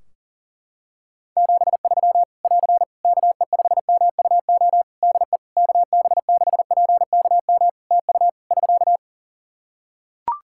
Morse code, heard as a keyed tone, about 42 wpm, 700 Hz. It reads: 73 <AR> KE5MUO DE KB6PKM TU <SK>
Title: Practice QSO - Exchange 4 - Ending - 40wpm